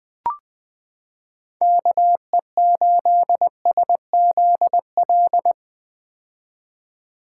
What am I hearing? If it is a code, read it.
Morse code, heard as a keyed tone, about 21 wpm, 700 Hz. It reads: KE8SZL
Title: 2023 POTA Activator Call Signs - 20wpm